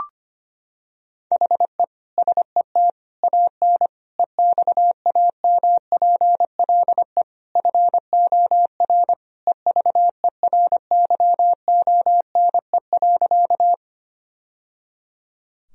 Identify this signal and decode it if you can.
Morse code, heard as a keyed tone, about 25 wpm, 700 Hz. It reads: HE SET AN EXAMPLE FOR EVERYONE.
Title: Sentences from Top 1000 Words - Review Part 1 - 25wpm